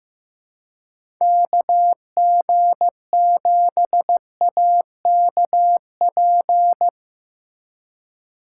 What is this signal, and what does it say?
Morse code, heard as a keyed tone, about 15 wpm, 700 Hz. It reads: KG7AKP
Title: USA Call Signs - 3x Character Spacing - 15wpm